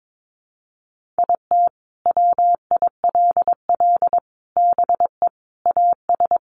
Morse code, heard as a keyed tone, about 22 wpm, 700 Hz. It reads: IT WILL BE AS
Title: Sentences from Top 1400 Words 22wpm